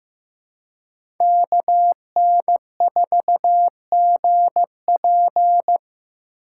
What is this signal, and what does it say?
Morse code, heard as a keyed tone, about 15 wpm, 700 Hz. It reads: KN4GP
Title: USA Call Signs - 5 Characters - 3x Character Spacing - 15wpm